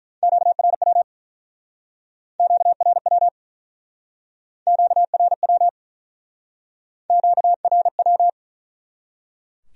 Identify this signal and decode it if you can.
Morse code, heard as a keyed tone, about 48 wpm, 700 Hz. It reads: QRW QRW QRW QRW